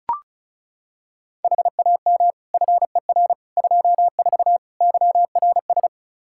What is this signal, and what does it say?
Morse code, heard as a keyed tone, about 36 wpm, 700 Hz. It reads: HAM FER 24 YRS
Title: Practice QSO - Exchange 3 - More Info - 35wpm